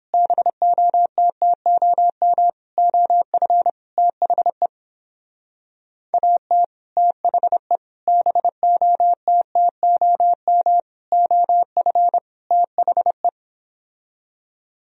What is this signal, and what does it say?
Morse code, heard as a keyed tone, about 29 wpm, 700 Hz. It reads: BOTTOM OF THE AT THE BOTTOM OF THE